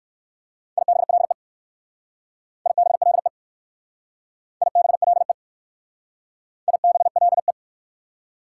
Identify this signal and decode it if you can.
Morse code, heard as a keyed tone, about 55 wpm, 700 Hz. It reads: IBLE IBLE IBLE IBLE
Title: Speed-Racing Prefixes and Suffixes 45wpm